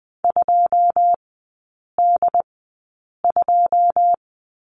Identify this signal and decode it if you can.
Morse code, heard as a keyed tone, about 20 wpm, spaced as for 8 wpm, 700 Hz. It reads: 2D2
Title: 081 Lesson 14e - Number 2 - Three Characters 20wpm